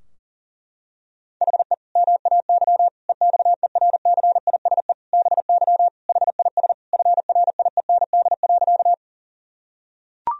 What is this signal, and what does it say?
Morse code, heard as a keyed tone, about 40 wpm, 700 Hz. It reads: HE MAY EXERCISE BY HIS FRIEND.
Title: Sentences from Top 1000 Words - Review Part 4 - 40wpm